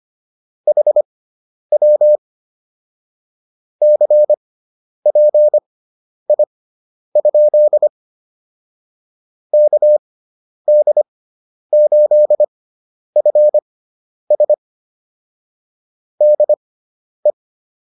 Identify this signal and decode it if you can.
Morse code, heard as a keyed tone, about 25 wpm, spaced as for 10 wpm, 600 Hz. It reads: HW CPI? KD8FS DE
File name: QSO107 10